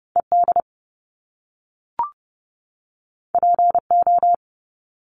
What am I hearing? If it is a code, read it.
Morse code, heard as a keyed tone, about 30 wpm, 700 Hz. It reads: ED PO
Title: Top Two Letter Combinations in Words 30wpm